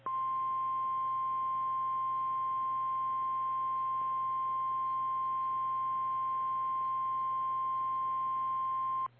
Emergency Alert System (SAME) tones (the 1050 Hz attention tone). It is heard from 1.1 to 8.8 s.
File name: eas / KKPRKXPN / KKPRKXPN